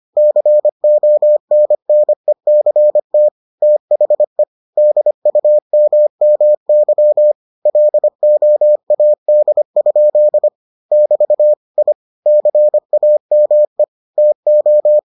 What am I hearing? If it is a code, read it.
Morse code, heard as a keyed tone, about 25 wpm, 600 Hz. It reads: CONNECT THE DUMMY LOAD? <BT> I CAME TO